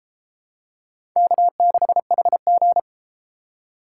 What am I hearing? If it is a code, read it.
Morse code, heard as a keyed tone, about 32 wpm, 700 Hz. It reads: K6HG